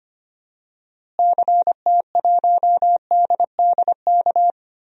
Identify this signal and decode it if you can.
Morse code, heard as a keyed tone, about 25 wpm, 700 Hz. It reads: CT1DDK
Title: DXCC-100 Call Signs - 6 Characters - Repeated 3x - 25wpm